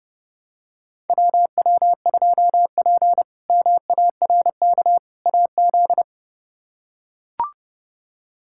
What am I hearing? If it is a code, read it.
Morse code, heard as a keyed tone, about 30 wpm, 700 Hz. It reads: WW2P MARK AZ
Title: K1USN SST QSO Practice - 30wpm